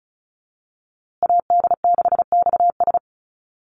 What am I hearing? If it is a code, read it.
Morse code, heard as a keyed tone, about 34 wpm, 700 Hz. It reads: AD6XS
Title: USA Call Signs Repeated 2x 35wpm